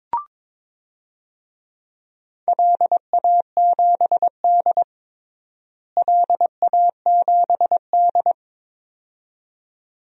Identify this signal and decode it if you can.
Morse code, heard as a keyed tone, about 21 wpm, 700 Hz. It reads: LA7D LA7D
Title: DXCC-100 Call Signs - 4 Characters - Repeated 2x - 22wpm